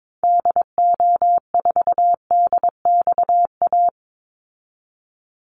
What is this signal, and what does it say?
Morse code, heard as a keyed tone, about 22 wpm, 700 Hz. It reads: DO4DXA
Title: Call Signs Repeated 3x - 22wpm